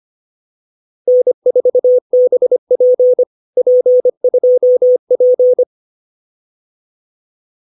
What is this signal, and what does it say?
Morse code, heard as a keyed tone, about 25 wpm, 560 Hz. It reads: N4BP P2P